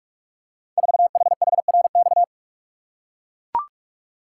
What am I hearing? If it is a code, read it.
Morse code, heard as a keyed tone, about 43 wpm, 700 Hz. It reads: 4HHRX
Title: Code Groups - 5 Letter:Numbers - 45wpm